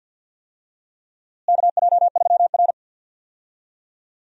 Morse code, heard as a keyed tone, about 51 wpm, 700 Hz. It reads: KJ3R
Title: USA Call Signs - 4 Characters - Repeated 3x - 50wpm